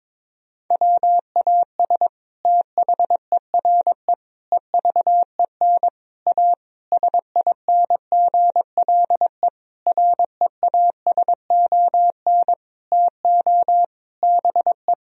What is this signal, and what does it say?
Morse code, heard as a keyed tone, about 22 wpm, 700 Hz. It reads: WAS THERE EVEN A SINGLE REASON TO BE
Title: Sentences from Top 1000 Words - Review Part 4 - 22wpm